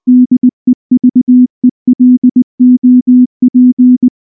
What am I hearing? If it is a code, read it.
Morse code, heard as a keyed tone, about 19 wpm, 250 Hz. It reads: DEVELOP
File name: morse / morse-training-20WPM-1